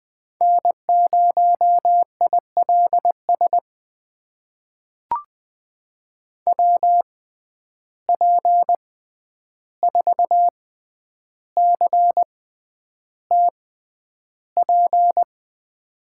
Morse code, heard as a keyed tone, about 20 wpm, spaced as for 10 wpm, 700 Hz. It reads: N0ILS WP4CTP
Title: USA Call Signs - 6x Character Spacing - 20wpm